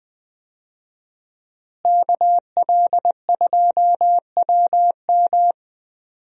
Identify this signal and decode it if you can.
Morse code, heard as a keyed tone, about 20 wpm, 700 Hz. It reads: KL2WM
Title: USA Call Signs - 5 Characters - Repeated 3x - 20wpm